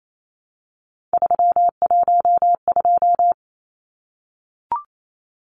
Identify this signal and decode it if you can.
Morse code, heard as a keyed tone, about 28 wpm, 700 Hz. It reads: 312
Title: Numbers - 3-Digit 28wpm